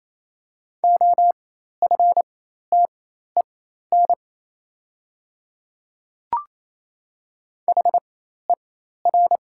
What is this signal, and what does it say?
Morse code, heard as a keyed tone, about 28 wpm, spaced as for 13 wpm, 700 Hz. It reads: OFTEN HER
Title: 4x Character Spacing - Top 300 Words - 28wpm